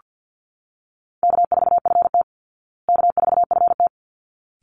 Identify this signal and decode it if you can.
Morse code, heard as a keyed tone, about 49 wpm, 700 Hz. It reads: K4FT K4FT
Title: Call Signs Repeated 3x - 50wpm